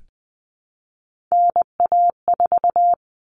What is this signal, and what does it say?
Morse code, heard as a keyed tone, about 20 wpm, 700 Hz. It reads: NA4
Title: All Cut Numbers - 3-Digit 20wpm